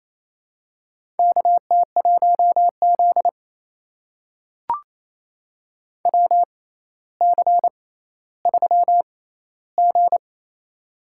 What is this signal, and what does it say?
Morse code, heard as a keyed tone, about 28 wpm, spaced as for 14 wpm, 700 Hz. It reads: KT1Z WC3G
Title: USA Call Signs - 4 Characters - 6x Character Spacing - 28wpm